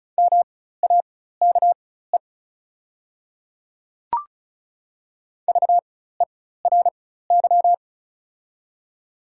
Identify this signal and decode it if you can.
Morse code, heard as a keyed tone, about 33 wpm, spaced as for 16 wpm, 690 Hz. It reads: MAKE VERY
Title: 4x Character Spacing - Top 300 Words - 35wpm